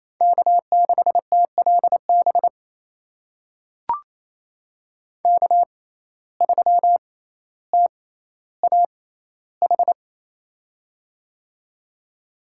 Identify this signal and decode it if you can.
Morse code, heard as a keyed tone, about 28 wpm, spaced as for 14 wpm, 700 Hz. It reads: K6TLB K3TAH
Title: USA Call Signs - 5 Characters - 6x Character Spacing - 28wpm